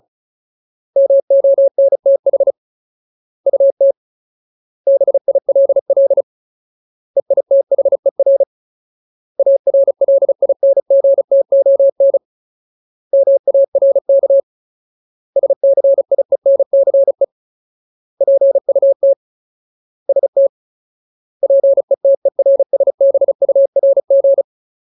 Morse code, heard as a keyed tone, about 34 wpm, 550 Hz. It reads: MONTH UT BILL EITHER ARLINGTON MARK SCIENCE PUT ST PETERSBURG
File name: Ultimate Word List / 35WPM Ultimate